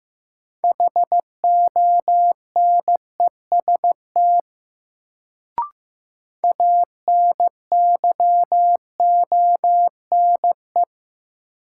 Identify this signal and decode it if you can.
Morse code, heard as a keyed tone, about 15 wpm, 700 Hz. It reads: HONEST ANYONE
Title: Sets of 5 Words Encore - New Word Drill - 15wpm